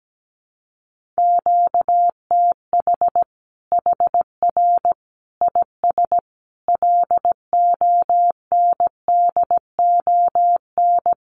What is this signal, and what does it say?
Morse code, heard as a keyed tone, about 17 wpm, 700 Hz. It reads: QTH HR IS LONDON
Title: Common QSO Phrases 17wpm